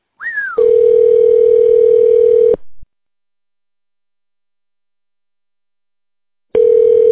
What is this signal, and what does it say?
Ringback tone (North America), 440 + 480 Hz.